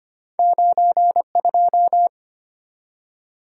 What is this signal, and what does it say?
Morse code, heard as a keyed tone, about 25 wpm, 690 Hz. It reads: 92
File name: Two Letter-Number Combinations - 25wpm